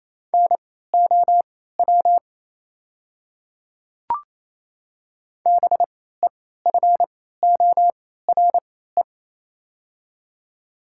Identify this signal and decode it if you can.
Morse code, heard as a keyed tone, about 27 wpm, spaced as for 16 wpm, 700 Hz. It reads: NOW BEFORE
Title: 3x Character Spacing - Top 200 Words - 28wpm